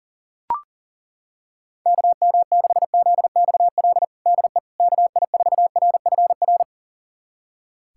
Morse code, heard as a keyed tone, about 40 wpm, 700 Hz. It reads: KM6ZXL DE KI4RFR
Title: USA Callsign DE Callsign 40wpm